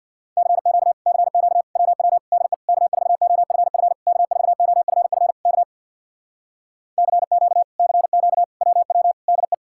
Morse code, heard as a keyed tone, about 64 wpm, 700 Hz. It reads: CQ CQ WW DE K4OFF K4OFF K CQ CQ WW DE